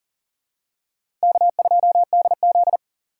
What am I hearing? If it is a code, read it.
Morse code, heard as a keyed tone, about 40 wpm, 700 Hz. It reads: K2DZ
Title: USA Call Signs - 4 Characters - 3x Character Spacing - 40wpm